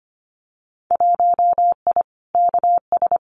Morse code, heard as a keyed tone, about 25 wpm, 700 Hz. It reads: 1I KS
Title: Winter Field Day Exchanges Encore 25wpm